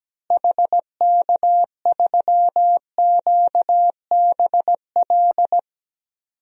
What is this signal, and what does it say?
Morse code, heard as a keyed tone, about 17 wpm, 700 Hz. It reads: HK3QBL
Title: DXCC-100 Call Signs - 6 Characters - Repeated 2x - 17wpm